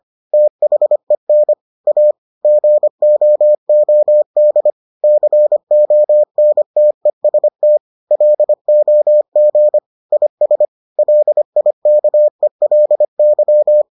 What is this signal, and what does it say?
Morse code, heard as a keyed tone, about 25 wpm, 600 Hz. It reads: THEN A GOOD CONTEST LOG IS LIKELY